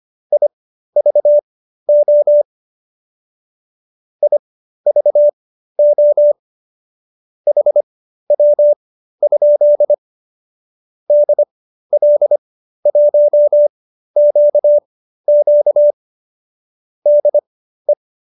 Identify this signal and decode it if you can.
Morse code, heard as a keyed tone, about 25 wpm, spaced as for 13 wpm, 590 Hz. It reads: IVO IVO HW? DL1QQ DE